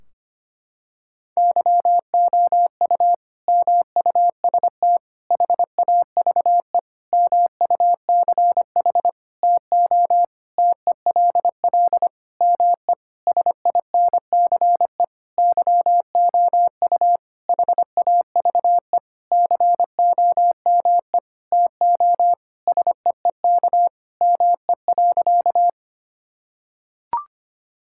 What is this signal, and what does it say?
Morse code, heard as a keyed tone, about 25 wpm, 700 Hz. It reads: YOU MUST HAVE MUCH TO TELL ME SINCE YOU HAVE COME TO SEEK ME.
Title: Sentences from Top 1600 Words 25wpm